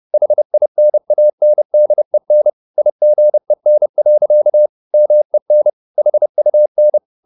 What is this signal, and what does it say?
Morse code, heard as a keyed tone, about 30 wpm, 600 Hz. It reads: HINANDEN IGEN. MEN HUN
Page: Ferie 30 wpm